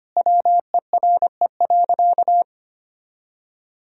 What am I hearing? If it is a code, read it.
Morse code, heard as a keyed tone, about 25 wpm, 700 Hz. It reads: WERE.
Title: Sentences from Top 1000 Words - Review Part 1 - 25wpm